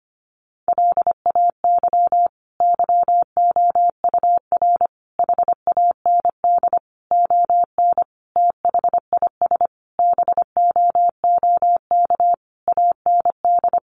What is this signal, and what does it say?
Morse code, heard as a keyed tone, about 25 wpm, 700 Hz. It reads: LAY YOUR HAND ON THIS BOOK AND
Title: Sentences from Top 400 Words 25wpm